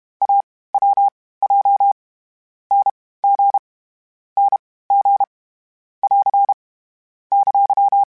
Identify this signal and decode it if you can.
Morse code, heard as a keyed tone, about 32 wpm, spaced as for 18 wpm, 800 Hz. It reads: AWJ NG NG <AR> !
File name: Short Runs / Short Runs-mp3s / Unballanced @30